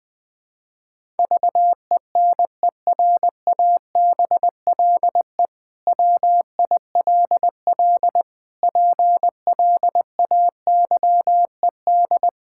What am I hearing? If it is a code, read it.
Morse code, heard as a keyed tone, about 21 wpm, 700 Hz. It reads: VENERABLE WILL PLAYED
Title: Pangram Challenge v2 - 20wpm